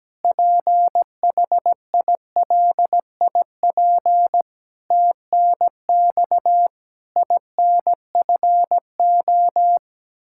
Morse code, heard as a keyed tone, about 17 wpm, 700 Hz. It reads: PHILIP TNX INFO
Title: Practice QSO - Exchange 3 - More Info - 17wpm